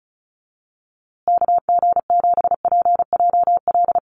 Morse code, heard as a keyed tone, about 34 wpm, 700 Hz. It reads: KG7PJL
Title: USA Call Signs Repeated 3x 35wpm